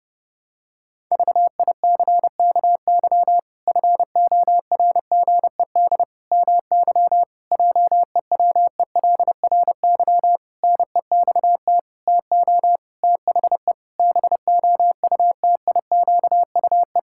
Morse code, heard as a keyed tone, about 32 wpm, 700 Hz. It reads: VICKY FORGED MY JEWELRY NEXT TO THE BOUTIQUE
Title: Pangram Challenge v2 - 30wpm